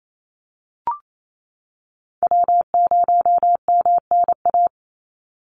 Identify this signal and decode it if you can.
Morse code, heard as a keyed tone, about 28 wpm, 700 Hz. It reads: W0MNA
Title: NA SOTA Chaser Callsigns 28wpm